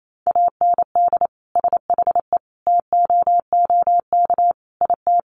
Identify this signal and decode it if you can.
Morse code, heard as a keyed tone, about 28 wpm, 700 Hz. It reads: AND SHE TOOK IT
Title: Sentences from Top 1400 Words 28wpm